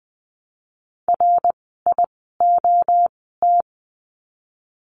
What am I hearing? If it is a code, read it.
Morse code, heard as a keyed tone, about 20 wpm, spaced as for 14 wpm, 700 Hz. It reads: RIOT
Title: 054 Lesson 10f - Letter r - Words 20wpm